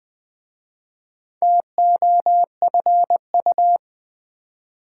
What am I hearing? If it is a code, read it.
Morse code, heard as a keyed tone, about 20 wpm, 700 Hz. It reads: TOFU
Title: Difficult Words with Hints - 20wpm